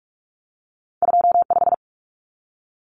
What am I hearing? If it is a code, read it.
Morse code, heard as a keyed tone, about 46 wpm, 700 Hz. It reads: 25
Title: Common Cut Numbers - 2-Digit 45wpm